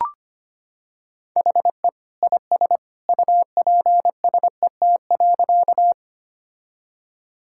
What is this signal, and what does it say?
Morse code, heard as a keyed tone, about 25 wpm, 700 Hz. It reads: HE IS UPSET.
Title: Sentences from Top 1000 Words - Review Part 5 - 25wpm